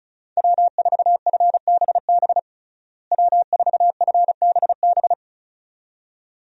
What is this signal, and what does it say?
Morse code, heard as a keyed tone, about 34 wpm, 700 Hz. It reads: W4FBB W4FBB
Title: USA Call Signs - 5 Characters - Repeated 2x - 35wpm